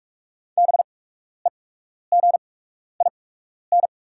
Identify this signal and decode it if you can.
Morse code, heard as a keyed tone, about 44 wpm, spaced as for 12 wpm, 690 Hz. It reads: BEGIN
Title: 8x Character Spacing - Top 300 Words - 45wpm